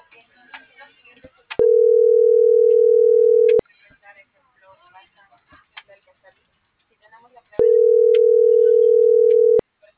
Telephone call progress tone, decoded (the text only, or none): ringback tone